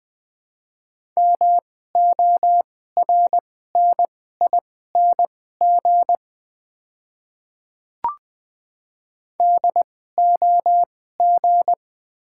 Morse code, heard as a keyed tone, about 20 wpm, spaced as for 14 wpm, 690 Hz. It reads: MORNING DOG
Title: 2x Character Spacing - Top 500 Words - 20wpm